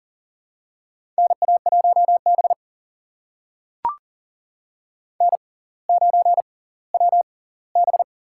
Morse code, heard as a keyed tone, about 40 wpm, spaced as for 20 wpm, 700 Hz. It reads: NA1B N9WB
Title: USA Call Signs - 4 Characters - 6x Character Spacing - 40wpm